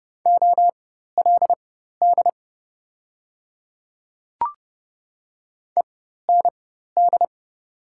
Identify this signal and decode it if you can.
Morse code, heard as a keyed tone, about 30 wpm, spaced as for 14 wpm, 700 Hz. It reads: OLD END
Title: 4x Character Spacing - Top 200 Words - 30wpm